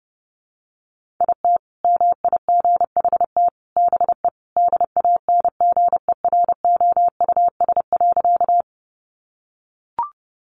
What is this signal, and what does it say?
Morse code, heard as a keyed tone, about 30 wpm, 700 Hz. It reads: IT MIGHT BE DANGEROUS.
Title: Sentences from Top 1200 Words 30wpm